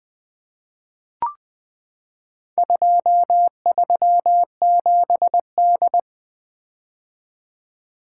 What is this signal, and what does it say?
Morse code, heard as a keyed tone, about 20 wpm, 700 Hz. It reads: 237D
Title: Four Letter-Number Combinations - 20wpm